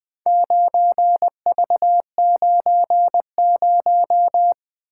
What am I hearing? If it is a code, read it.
Morse code, heard as a keyed tone, about 20 wpm, 700 Hz. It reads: 9V90
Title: Four Letter-Number Combinations - 20wpm